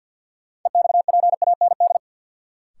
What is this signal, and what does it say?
Morse code, heard as a keyed tone, about 53 wpm, 700 Hz. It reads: EXPAND
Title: Sentences from Top 1400 Words - New Word Drill - 50wpm